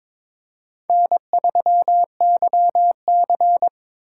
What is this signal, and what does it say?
Morse code, heard as a keyed tone, about 22 wpm, 700 Hz. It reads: N3YC
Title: USA Call Signs - 4 Characters - 2x Character Spacing - 22wpm